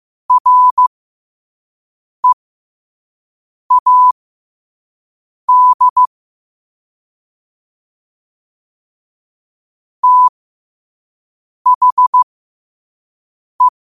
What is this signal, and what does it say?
Morse code, heard as a keyed tone, about 15 wpm, spaced as for 5 wpm, 1000 Hz. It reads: READ THE
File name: qotdfiles / 5wpm